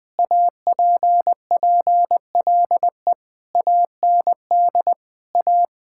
Morse code, heard as a keyed tone, about 20 wpm, 700 Hz. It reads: APPLE AND A
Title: Sentences with 2-Word Phrases - 20wpm